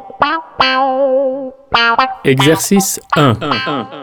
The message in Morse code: none